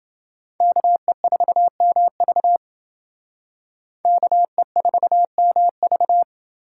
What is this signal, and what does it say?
Morse code, heard as a keyed tone, about 28 wpm, 700 Hz. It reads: KE4MV KE4MV